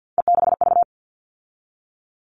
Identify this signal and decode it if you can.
Morse code, heard as a keyed tone, about 48 wpm, 700 Hz. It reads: E6V
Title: All Cut Numbers - 3-Digit 50wpm